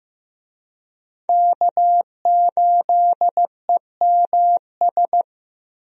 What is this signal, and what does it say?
Morse code, heard as a keyed tone, about 15 wpm, 700 Hz. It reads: K8EMS
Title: USA Call Signs - 5 Characters - Repeated 3x - 15wpm